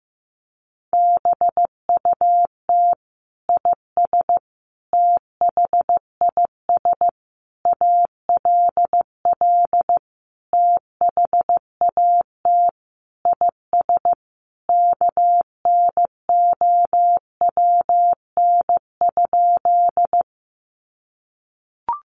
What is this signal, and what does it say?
Morse code, heard as a keyed tone, about 15 wpm, 700 Hz. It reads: BUT IS THIS ALL THAT IS KNOWN?
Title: Sentences from Top 600 Words 15wpm